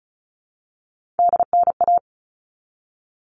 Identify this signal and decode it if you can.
Morse code, heard as a keyed tone, about 34 wpm, 700 Hz. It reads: DNA
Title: All Cut Numbers - 3-Digit 35wpm